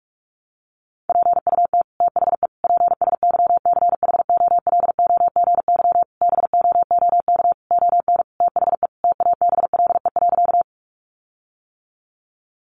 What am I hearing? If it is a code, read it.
Morse code, heard as a keyed tone, about 44 wpm, 700 Hz. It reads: PUT THE PSYCHOLOGY BOOK ON THE TABLE.
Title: Sentences from Top 1400 Words 45wpm